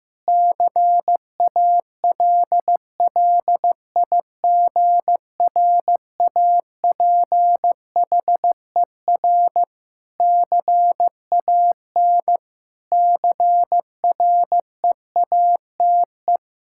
Morse code, heard as a keyed tone, about 15 wpm, 700 Hz. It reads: CALLIGRAPHER CAN CREATE
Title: Pangram Challenge v2 - 15wpm